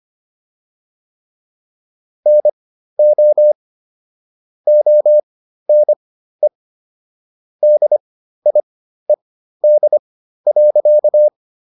Morse code, heard as a keyed tone, about 25 wpm, spaced as for 13 wpm, 600 Hz. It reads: NO ONE DIED.